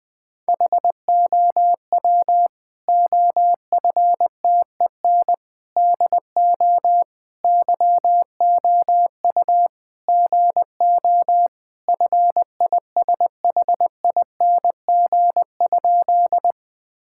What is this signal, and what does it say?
Morse code, heard as a keyed tone, about 21 wpm, 700 Hz. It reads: HOW OFTEN DO YOU GO FISHING?
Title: Sentences from Top 1000 Words - Review Part 5 - 20wpm